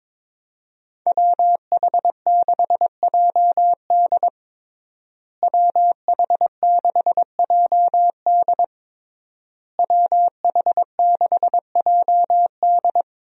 Morse code, heard as a keyed tone, about 21 wpm, 700 Hz. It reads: WH6JD WH6JD WH6JD
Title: USA Call Signs - 5 Characters - Repeated 3x - 22wpm